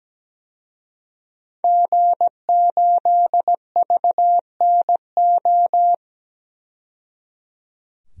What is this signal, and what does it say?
Morse code, heard as a keyed tone, about 17 wpm, 700 Hz. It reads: G8VNO
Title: DXCC-100 Call Signs - 5 Characters - 17wpm